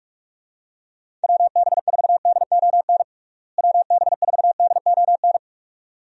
Morse code, heard as a keyed tone, about 46 wpm, 700 Hz. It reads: WB4DON WB4DON
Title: USA Call Signs - 6 Characters - Repeated 2x - 45wpm